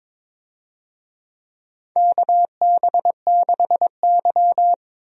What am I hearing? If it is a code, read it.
Morse code, heard as a keyed tone, about 22 wpm, 700 Hz. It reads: KB6Y
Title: USA Call Signs - 4 Characters - Repeated 3x - 22wpm